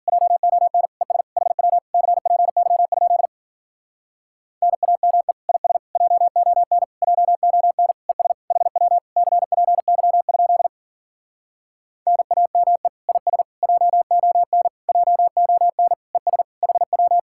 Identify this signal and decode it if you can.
Morse code, heard as a keyed tone, about 53 wpm, 700 Hz. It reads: JON ES HW CPY? NAME IS JON JON ES HW CPY? NAME IS JON JON ES HW